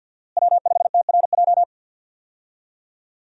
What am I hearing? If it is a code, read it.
Morse code, heard as a keyed tone, about 52 wpm, 700 Hz. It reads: W5TRJ
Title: USA Call Signs - 5 Characters - Repeated 3x - 50wpm